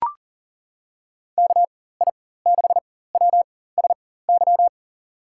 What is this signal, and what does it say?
Morse code, heard as a keyed tone, about 40 wpm, spaced as for 19 wpm, 700 Hz. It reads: KI6WSY
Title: USA Call Signs - 6 Characters - 4x Character Spacing - 40wpm